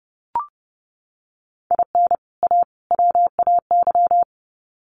Morse code, heard as a keyed tone, about 30 wpm, 700 Hz. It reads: IN A WAY
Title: Sets of 3 Words 30wpm